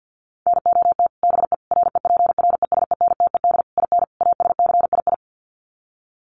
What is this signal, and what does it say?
Morse code, heard as a keyed tone, about 49 wpm, 700 Hz. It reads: NOT BE REPRESENTED IN ASCII
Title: Sets of 5 Words 50wpm